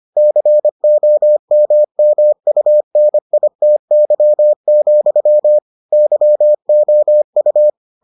Morse code, heard as a keyed tone, about 25 wpm, 600 Hz. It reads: COMMUNITY, YOU